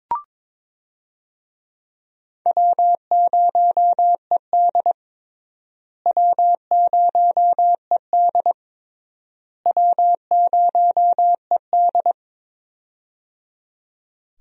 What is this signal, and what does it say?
Morse code, heard as a keyed tone, about 21 wpm, 700 Hz. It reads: W0ED W0ED W0ED
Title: USA Call Signs - 4 Characters - Repeated 3x - 22wpm